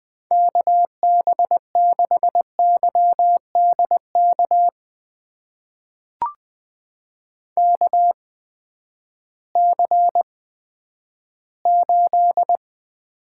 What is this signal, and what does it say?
Morse code, heard as a keyed tone, about 20 wpm, spaced as for 11 wpm, 700 Hz. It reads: KB6YDK KC8
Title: USA Call Signs - 6 Characters - 8x Character Spacing - 20wpm